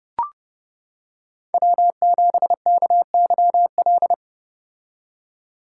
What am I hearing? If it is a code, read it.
Morse code, heard as a keyed tone, about 29 wpm, 700 Hz. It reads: W7KYL